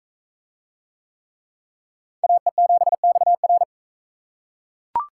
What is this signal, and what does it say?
Morse code, heard as a keyed tone, about 41 wpm, 700 Hz. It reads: AE7XR